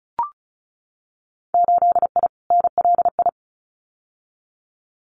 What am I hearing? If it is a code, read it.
Morse code, heard as a keyed tone, about 36 wpm, 700 Hz. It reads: 8I NLI
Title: Winter Field Day Exchanges Encore 35wpm